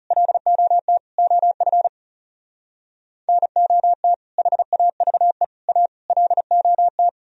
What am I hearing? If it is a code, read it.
Morse code, heard as a keyed tone, about 37 wpm, 700 Hz. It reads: LOT OF NOT HAVE A LOT
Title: Speed-Racing - Sets of 5 Words 30wpm